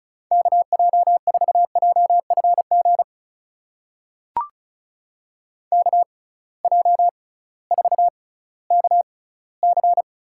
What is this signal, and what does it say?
Morse code, heard as a keyed tone, about 36 wpm, spaced as for 19 wpm, 700 Hz. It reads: KJ4JFG KJ4KC
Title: USA Call Signs - 6 Characters - 6x Character Spacing - 35wpm